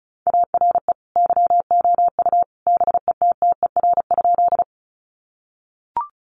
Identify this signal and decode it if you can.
Morse code, heard as a keyed tone, about 34 wpm, 700 Hz. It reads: ARE YOU BETTER?
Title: Sentences from Top 400 Words 35wpm